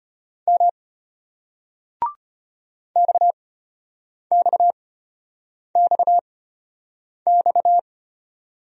Morse code, heard as a keyed tone, about 35 wpm, spaced as for 8 wpm, 700 Hz. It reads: M XXXX